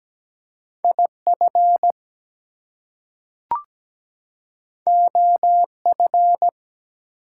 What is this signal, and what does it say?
Morse code, heard as a keyed tone, about 17 wpm, 700 Hz. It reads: IF OF
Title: Two Letter Words - 17wpm